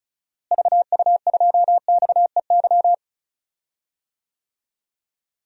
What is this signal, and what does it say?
Morse code, heard as a keyed tone, about 34 wpm, 700 Hz. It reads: VU2XEY
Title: DXCC-100 Call Signs - 6 Characters - Repeated 2x - 35wpm